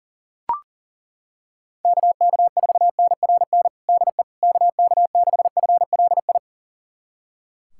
Morse code, heard as a keyed tone, about 40 wpm, 700 Hz. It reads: KK4NRN DE KK6FLI
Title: USA Callsign DE Callsign - 6 Characters - 40wpm